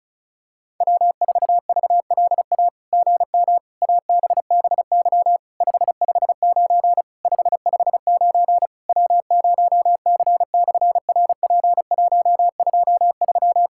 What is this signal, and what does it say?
Morse code, heard as a keyed tone, about 36 wpm, 700 Hz. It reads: W4VLA GM ABBY 559 559 W0C/RP123
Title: SOTA QSO Practice 35wpm